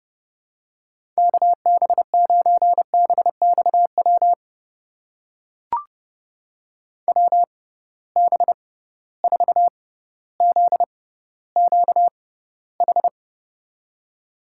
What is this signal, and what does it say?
Morse code, heard as a keyed tone, about 30 wpm, spaced as for 15 wpm, 700 Hz. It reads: KB9BXW WB4ZQH
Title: USA Call Signs - 6 Characters - 6x Character Spacing - 30wpm